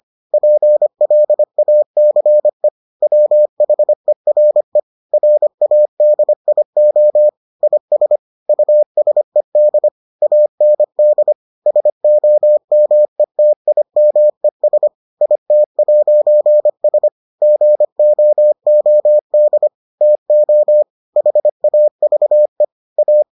Morse code, heard as a keyed tone, about 25 wpm, 600 Hz. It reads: PLACE WHERE RADIO IS USED AND SOMETIMES IT'S GOOD TO HAVE A